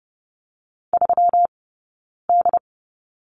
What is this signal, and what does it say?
Morse code, heard as a keyed tone, about 30 wpm, 700 Hz. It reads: 3 D
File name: Lesson 20d - Number 3 - Two Characters 30wpm